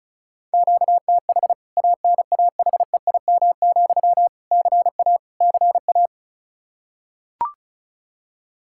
Morse code, heard as a keyed tone, about 36 wpm, 700 Hz. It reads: QTH ANAHEIM, CA CA
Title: K1USN SST QSO Practice 35wpm